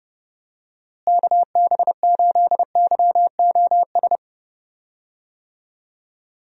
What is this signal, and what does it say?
Morse code, heard as a keyed tone, about 31 wpm, 700 Hz. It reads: KB8YOS
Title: USA Call Signs - 6 Characters - Repeated 2x - 30wpm